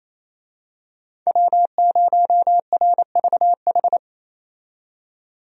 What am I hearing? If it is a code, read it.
Morse code, heard as a keyed tone, about 28 wpm, 700 Hz. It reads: W0RVH
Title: USA Call Signs - 5 Characters - 28wpm